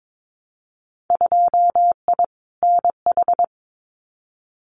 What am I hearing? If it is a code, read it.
Morse code, heard as a keyed tone, about 22 wpm, 700 Hz. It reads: 2I NH
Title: Winter Field Day Exchanges Encore 22wpm